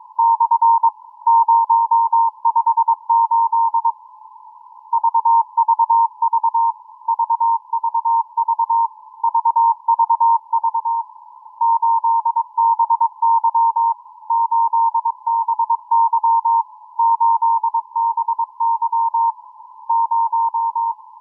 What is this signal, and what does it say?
Morse code, heard as a keyed tone, about 22 wpm, 920 Hz. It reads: / 058 VVV VVV VVV 8BY 8BY 8BY 0